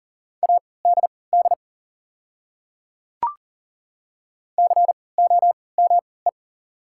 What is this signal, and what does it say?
Morse code, heard as a keyed tone, about 39 wpm, spaced as for 22 wpm, 700 Hz. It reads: ADD COME
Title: 3x Character Spacing - Top 200 Words - 40wpm